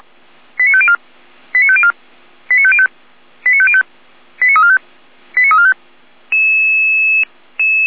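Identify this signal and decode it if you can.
Five-tone selective call (ZVEI), heard as quick runs of five tones, tones of about 70 ms. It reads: 89504 89504 89505 89505 89345 89345
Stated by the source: Five-tone ZVEI selective call 89135 received by SDR.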